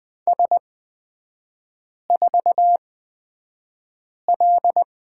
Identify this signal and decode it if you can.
Morse code, heard as a keyed tone, about 20 wpm, spaced as for 5 wpm, 690 Hz. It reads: S4L
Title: 229 Lesson 37e - Letter x - Three Characters 20wpm